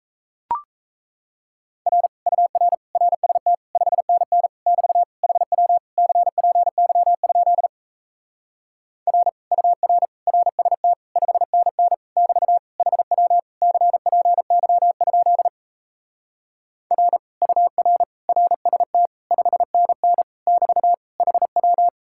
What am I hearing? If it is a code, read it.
Morse code, heard as a keyed tone, about 37 wpm, 700 Hz. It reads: R UR RST 5NN <BT> HW CPY? R UR RST 5NN <BT> HW CPY? R UR RST 5NN <BT> HW